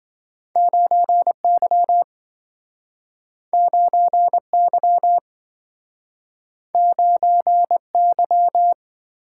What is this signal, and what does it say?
Morse code, heard as a keyed tone, about 27 wpm, 700 Hz. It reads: 9Y 9Y 9Y